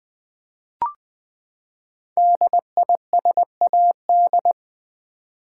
Morse code, heard as a keyed tone, about 20 wpm, 700 Hz. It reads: DISAD
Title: Build A Word - 5 Letters At A Time - 12-Letter Words - 20wpm